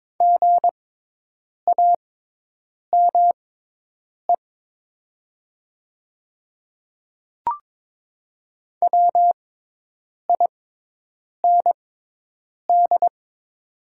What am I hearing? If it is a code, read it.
Morse code, heard as a keyed tone, about 22 wpm, spaced as for 8 wpm, 700 Hz. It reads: GAME WIND
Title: 6x Character Spacing - Top 500 Words - 22wpm